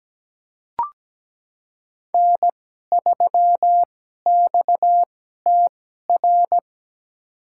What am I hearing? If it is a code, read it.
Morse code, heard as a keyed tone, about 17 wpm, spaced as for 12 wpm, 700 Hz. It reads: N3XTR
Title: USA Call Signs - 5 Characters - 2x Character Spacing - 17wpm